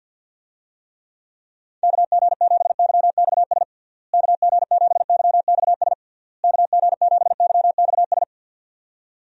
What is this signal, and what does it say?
Morse code, heard as a keyed tone, about 51 wpm, 700 Hz. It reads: KG7YXS KG7YXS KG7YXS
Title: USA Call Signs - 6 Characters - Repeated 3x - 50wpm